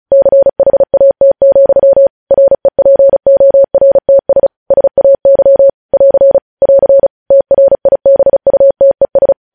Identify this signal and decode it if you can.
Morse code, heard as a keyed tone, about 36 wpm, 550 Hz. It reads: CHAT, REPORTS SAY <AR> <AR> TRIBUTES
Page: News Headlines in Morse Code at 35 WPM